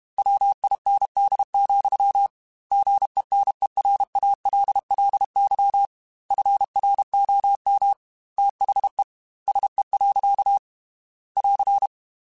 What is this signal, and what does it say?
Morse code, heard as a keyed tone, about 32 wpm, 800 Hz. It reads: WIND, GENERALLY FROM THE SE. <AR>
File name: Weather Sets / Weather Sets-mp3s / Weather 4 17 @30